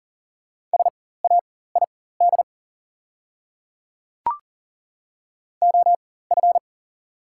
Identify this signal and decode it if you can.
Morse code, heard as a keyed tone, about 37 wpm, spaced as for 18 wpm, 690 Hz. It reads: SAID OF
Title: 4x Character Spacing - Top 100 Words - 40wpm